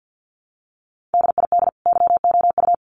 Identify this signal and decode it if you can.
Morse code, heard as a keyed tone, about 50 wpm, 700 Hz. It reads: DID YOU
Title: Sentences from Top 1200 Words 50wpm